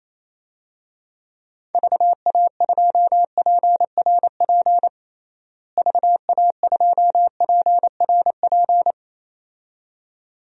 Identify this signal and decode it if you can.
Morse code, heard as a keyed tone, about 27 wpm, 700 Hz. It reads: VA2PRP VA2PRP
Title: DXCC-100 Call Signs - 6 Characters - Repeated 2x - 28wpm